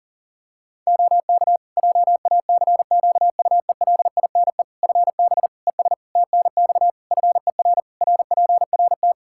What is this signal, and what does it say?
Morse code, heard as a keyed tone, about 40 wpm, 700 Hz. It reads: OK JACQUELINE FB ES TNX FER RPRT
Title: K1USN SST QSO Practice 40wpm